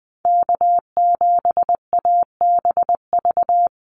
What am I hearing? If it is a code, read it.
Morse code, heard as a keyed tone, about 20 wpm, 700 Hz. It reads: K7ABV
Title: Call Signs Repeated 3x - 20wpm